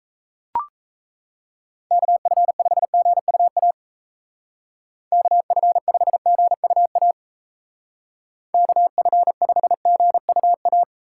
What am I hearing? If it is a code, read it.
Morse code, heard as a keyed tone, about 33 wpm, 700 Hz. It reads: KF5GUA KF5GUA KF5GUA
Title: Speed Racing - USA Call Signs 28wpm